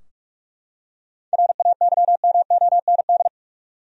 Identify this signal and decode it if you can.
Morse code, heard as a keyed tone, about 45 wpm, 700 Hz. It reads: RAYMOND
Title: USA Common Names - 45wpm